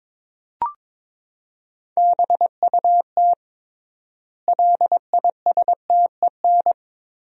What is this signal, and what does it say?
Morse code, heard as a keyed tone, about 22 wpm, 700 Hz. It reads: BUT LISTEN
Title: 3x Word-Spacing - Sentences from Top 400 Words - 22wpm